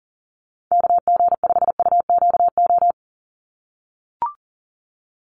Morse code, heard as a keyed tone, about 37 wpm, 700 Hz. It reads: KG5UQO
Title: USA Call Signs 40wpm